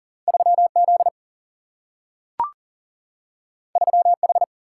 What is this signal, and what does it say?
Morse code, heard as a keyed tone, about 41 wpm, 700 Hz. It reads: 3Z 3H
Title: Single-Letter Single-Number 40wpm